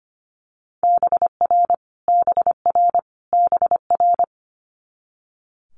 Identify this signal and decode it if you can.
Morse code, heard as a keyed tone, about 25 wpm, 700 Hz. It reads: BR BR BR
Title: Multi-letter Phonemes Repeated 3x 25wpm